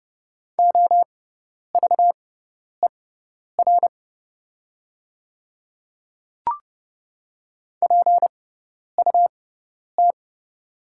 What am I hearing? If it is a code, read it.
Morse code, heard as a keyed tone, about 30 wpm, spaced as for 10 wpm, 700 Hz. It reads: OVER PUT
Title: 6x Character Spacing - Top 200 Words - 30wpm